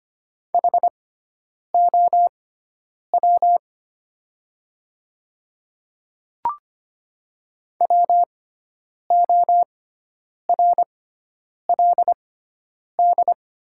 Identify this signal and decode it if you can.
Morse code, heard as a keyed tone, about 25 wpm, spaced as for 9 wpm, 700 Hz. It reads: HOW WORLD
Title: 6x Character Spacing - Top 200 Words - 25wpm